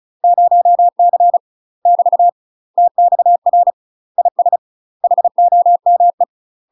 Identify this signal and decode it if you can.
Morse code, heard as a keyed tone, about 33 wpm, 700 Hz. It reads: 0C <BT> TXR IS HOME
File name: UK MASTER QSOs TRAINING / 35WPM / UK MASTER QSOs 0026